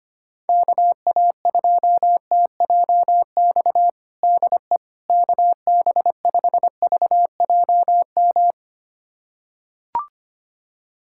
Morse code, heard as a keyed tone, about 25 wpm, 700 Hz. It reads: KA2TJX DE KB5VJM
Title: USA Callsign DE Callsign 25wpm